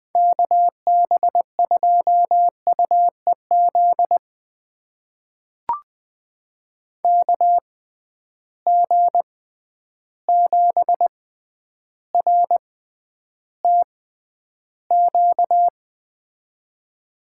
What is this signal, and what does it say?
Morse code, heard as a keyed tone, about 20 wpm, spaced as for 10 wpm, 700 Hz. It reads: KB2UEZ KG7RTQ
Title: USA Call Signs - 6 Characters - 6x Character Spacing - 20wpm